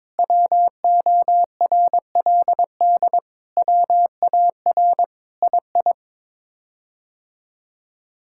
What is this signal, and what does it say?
Morse code, heard as a keyed tone, about 22 wpm, 700 Hz. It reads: WORLD WAR II
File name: practice / 22wpm / Sets of 7 Words - 22wpm